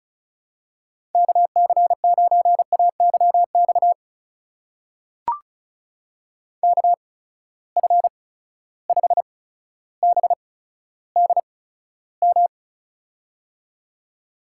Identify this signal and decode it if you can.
Morse code, heard as a keyed tone, about 36 wpm, spaced as for 15 wpm, 700 Hz. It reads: KC9AYX KF5BDM
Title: USA Call Signs - 6 Characters - 8x Character Spacing - 35wpm